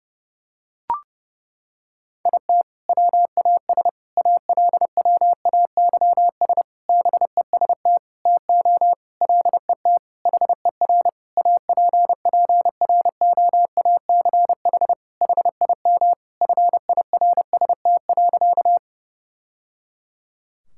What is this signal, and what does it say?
Morse code, heard as a keyed tone, about 30 wpm, 700 Hz. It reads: IT WAS ALWAYS BEST TO LET HER APPROACH HIM FIRST.
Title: Sentences from Top 1000 Words - Review Part 6 - 30wpm